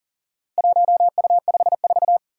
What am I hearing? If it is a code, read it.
Morse code, heard as a keyed tone, about 40 wpm, 700 Hz. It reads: 1U54
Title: Four Challenging Letter-Number Combinations 40wpm